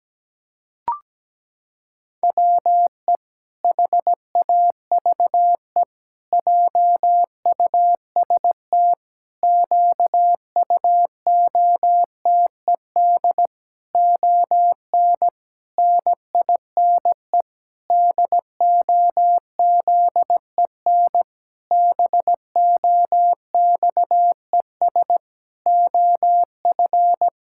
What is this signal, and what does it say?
Morse code, heard as a keyed tone, about 17 wpm, 700 Hz. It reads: WE HAVE JUST QUOTED ON NINE DOZEN BOXES OF
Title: Pangram Challenge v2 - 17wpm